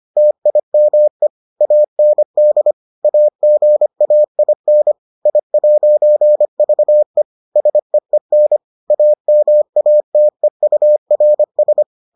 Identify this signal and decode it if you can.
Morse code, heard as a keyed tone, about 25 wpm, 600 Hz. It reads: TIME AND AGAIN I'VE SEEN AMATEURS